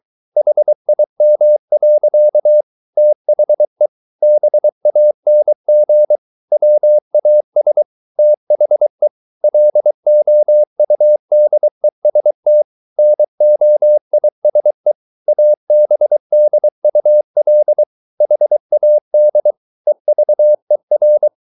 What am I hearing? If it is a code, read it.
Morse code, heard as a keyed tone, about 23 wpm, 600 Hz. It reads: HIM. THE BANG WAS THE LOUDEST NOISE ABDUL HAD EVER